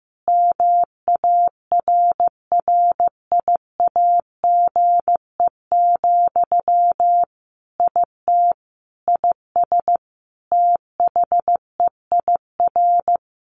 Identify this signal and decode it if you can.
Morse code, heard as a keyed tone, about 15 wpm, 700 Hz. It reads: MARRIAGE, IT IS THEIR
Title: Sentences from Top 1500 Words 15wpm